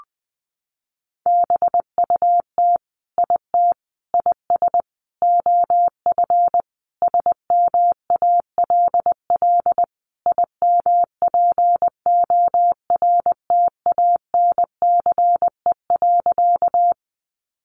Morse code, heard as a keyed tone, about 20 wpm, 700 Hz. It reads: BUT IT IS OF SMALL IMPORTANCE.
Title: Sentences from Top 1200 Words 20wpm